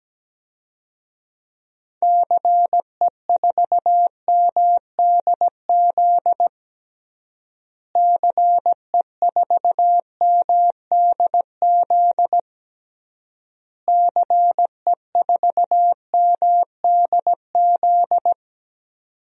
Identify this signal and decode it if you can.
Morse code, heard as a keyed tone, about 17 wpm, 700 Hz. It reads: CE4MDZ CE4MDZ CE4MDZ
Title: DXCC-100 Call Signs - 6 Characters - Repeated 3x - 17wpm